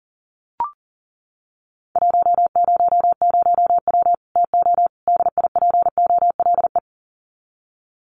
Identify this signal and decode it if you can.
Morse code, heard as a keyed tone, about 40 wpm, 700 Hz. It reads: 100W TO DIPOLE
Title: Common QSO Phrases 40wpm